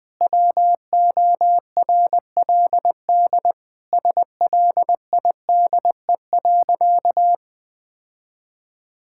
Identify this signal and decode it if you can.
Morse code, heard as a keyed tone, about 21 wpm, 700 Hz. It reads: WORLD SLIDE.
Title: Sentences from Top 1600 Words 20wpm